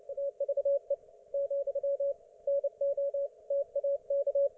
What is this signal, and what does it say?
Morse code, heard as a keyed tone, about 29 wpm, 560 Hz. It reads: TVE , NO TAK